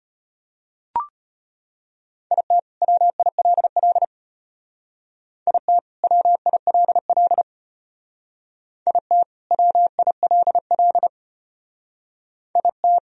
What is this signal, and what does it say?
Morse code, heard as a keyed tone, about 31 wpm, 700 Hz. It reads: IT WILL IT WILL IT WILL IT
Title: Speed-Racing - Sets of 2 Words 25wpm